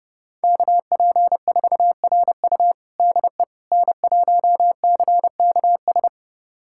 Morse code, heard as a keyed tone, about 31 wpm, 700 Hz. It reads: KP4RU DE N1CKS
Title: USA Callsign DE Callsign - 5 Characters - 30wpm